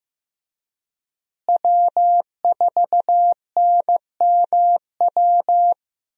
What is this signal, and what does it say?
Morse code, heard as a keyed tone, about 15 wpm, 700 Hz. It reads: W4NMW
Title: USA Call Signs - 5 Characters - Repeated 3x - 15wpm